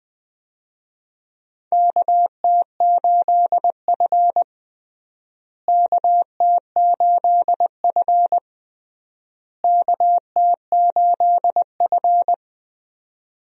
Morse code, heard as a keyed tone, about 20 wpm, 700 Hz. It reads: KT8F KT8F KT8F
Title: USA Call Signs - 4 Characters - Repeated 3x - 20wpm